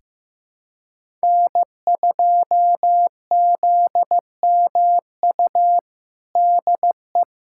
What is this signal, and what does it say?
Morse code, heard as a keyed tone, about 15 wpm, 700 Hz. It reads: N2ZMU DE
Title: USA Callsign DE Callsign 15wpm